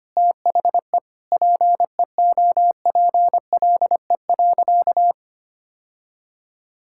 Morse code, heard as a keyed tone, about 25 wpm, 700 Hz. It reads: THE PEOPLE.
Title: Sentences from Top 1600 Words 25wpm